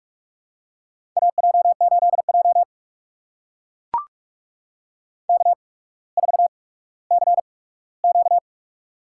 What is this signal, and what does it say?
Morse code, heard as a keyed tone, about 44 wpm, spaced as for 19 wpm, 690 Hz. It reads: AJ8J K4CQ
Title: USA Call Signs - 4 Characters - 8x Character Spacing - 45wpm